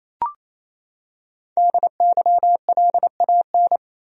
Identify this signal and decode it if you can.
Morse code, heard as a keyed tone, about 28 wpm, 700 Hz. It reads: DYLAN
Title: USA Common Names - Max 5 Letters - 28wpm